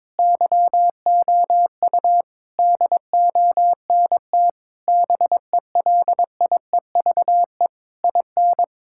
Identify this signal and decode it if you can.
Morse code, heard as a keyed tone, about 22 wpm, 700 Hz. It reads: YOU DONT BELIEVE IN
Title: Sentences from Top 1000 Words - Review Part 6 - 22wpm